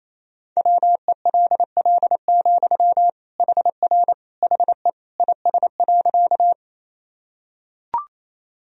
Morse code, heard as a keyed tone, about 28 wpm, 700 Hz. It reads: WELL, HR HE IS.
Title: Sentences from Top 200 Words and Abbreviations 28wpm